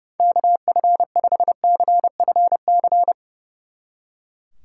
Morse code, heard as a keyed tone, about 38 wpm, 700 Hz. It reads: KF5CFC